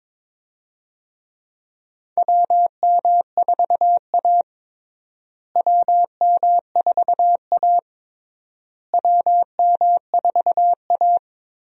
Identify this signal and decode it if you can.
Morse code, heard as a keyed tone, about 22 wpm, 700 Hz. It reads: WM4A WM4A WM4A
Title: USA Call Signs - 4 Characters - Repeated 3x - 22wpm